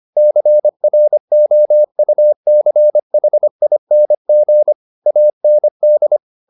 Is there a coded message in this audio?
Yes: Morse code from 0.2 to 6.2 s, heard as a keyed tone, about 25 wpm, 600 Hz. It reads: CROUCHING AND